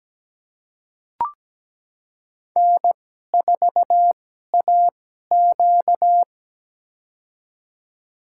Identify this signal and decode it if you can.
Morse code, heard as a keyed tone, about 17 wpm, spaced as for 12 wpm, 700 Hz. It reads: N4AQ
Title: USA Call Signs - 4 Characters - 2x Character Spacing - 17wpm